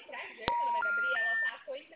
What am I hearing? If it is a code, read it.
Special information tone, with speech.